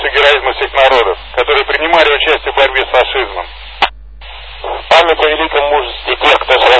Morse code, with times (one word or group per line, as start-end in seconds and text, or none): none